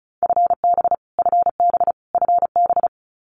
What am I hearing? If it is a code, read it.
Morse code, heard as a keyed tone, about 34 wpm, 700 Hz. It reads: FB FB FB
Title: Abbreviations Repeated 3x - 35wpm